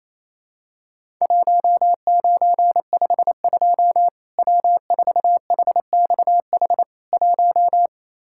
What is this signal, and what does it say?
Morse code, heard as a keyed tone, about 28 wpm, 700 Hz. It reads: 1952 W4HXH 1
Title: Worked All Europe DX Contest - QTC Exchange - 28wpm